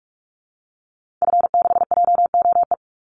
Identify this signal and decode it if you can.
Morse code, heard as a keyed tone, about 44 wpm, 700 Hz. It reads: F6JOE
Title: Call Signs - 45wpm